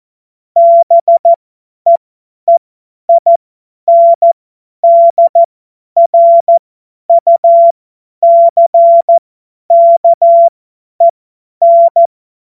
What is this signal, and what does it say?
Morse code, heard as a keyed tone, about 14 wpm, spaced as for 10 wpm, 680 Hz. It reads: BEEINDRUCKEN
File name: Morsekurs / Witze / Witz 23. Dezember 2025 50 F70